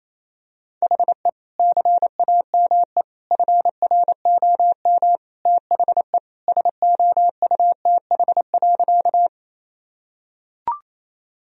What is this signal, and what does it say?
Morse code, heard as a keyed tone, about 28 wpm, 700 Hz. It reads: HE CAME FROM THE SOUTH.
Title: Sentences from Top 1000 Words - Review Part 2 - 28wpm